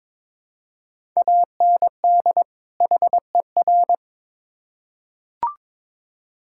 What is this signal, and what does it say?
Morse code, heard as a keyed tone, about 22 wpm, 700 Hz. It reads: AND HER
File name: Build A Phrase - 2-Word Phrases - 22wpm